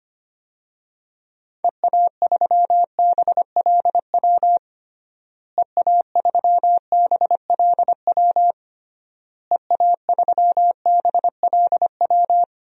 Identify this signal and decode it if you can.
Morse code, heard as a keyed tone, about 25 wpm, 700 Hz. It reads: EA3BLW EA3BLW EA3BLW
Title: DXCC-100 Call Signs - 6 Characters - Repeated 3x - 25wpm